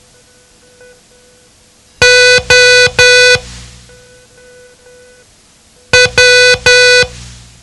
Morse code, heard as a keyed tone, about 10 wpm, 500 Hz. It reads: O W
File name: ham radio / CodeQuick / Code Quick 07